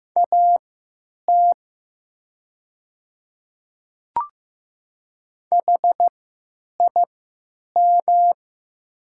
Morse code, heard as a keyed tone, about 15 wpm, spaced as for 9 wpm, 690 Hz. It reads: AT HIM
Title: 3x Character Spacing - Top 100 Words - 15wpm